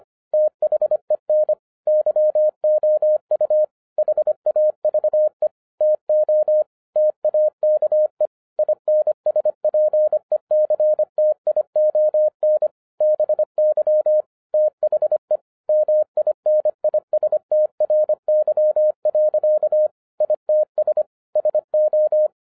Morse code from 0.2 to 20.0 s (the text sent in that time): THEN YOU HAVE TO TAKE INSPECTION BY THE MINISTRY.